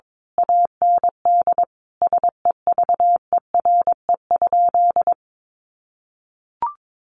Morse code, heard as a keyed tone, about 22 wpm, 700 Hz. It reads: AND SEVERE?
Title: Sentences from Top 1400 Words 22wpm